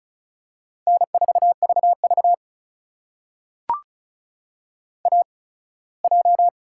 Morse code, heard as a keyed tone, about 36 wpm, spaced as for 21 wpm, 700 Hz. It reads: N4VV AJ
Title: USA Call Signs - 8x Character Spacing - 35wpm